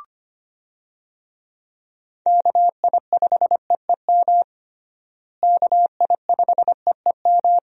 Morse code, heard as a keyed tone, about 25 wpm, 700 Hz. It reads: KI5EEM KI5EEM
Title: USA Call Signs - 6 Characters - Repeated 2x - 25wpm